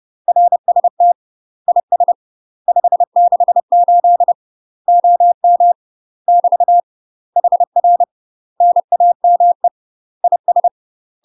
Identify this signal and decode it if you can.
Morse code, heard as a keyed tone, about 30 wpm, 700 Hz. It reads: RST IS 568 OM <BT> HR NAME IS